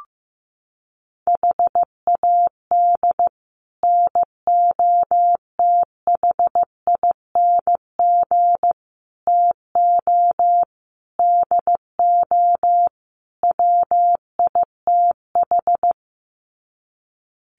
Morse code, heard as a keyed tone, about 15 wpm, 700 Hz. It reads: HAD NOTHING TO DO WITH
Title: Sets of 5 Words 15wpm